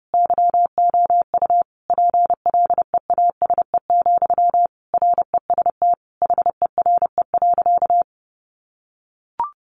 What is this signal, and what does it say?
Morse code, heard as a keyed tone, about 30 wpm, 700 Hz. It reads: YOU PLEASE, REST HERE.
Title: Sentences from Top 1500 Words 30wpm